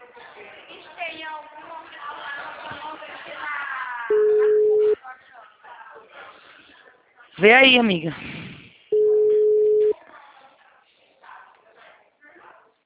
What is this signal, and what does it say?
Ringback tone (Europe, 425 Hz), with speech.